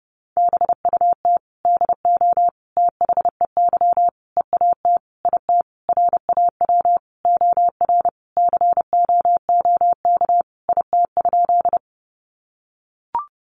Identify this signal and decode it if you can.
Morse code, heard as a keyed tone, about 30 wpm, 700 Hz. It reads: BUT DO THEY EAT IT RAW OR COOK IT?
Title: Sentences from Top 1500 Words 30wpm